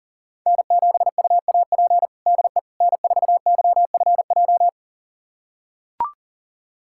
Morse code, heard as a keyed tone, about 40 wpm, 700 Hz. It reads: N7UAP DE N4YFJ
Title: USA Callsign DE Callsign 40wpm